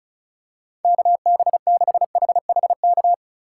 Morse code, heard as a keyed tone, about 36 wpm, 700 Hz. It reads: KB6HHK
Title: USA Call Signs - 6 Characters - 2x Character Spacing - 35wpm